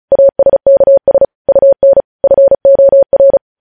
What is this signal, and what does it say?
Morse code, heard as a keyed tone, about 36 wpm, 550 Hz. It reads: ASKS UN FOR
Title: News Headlines at 35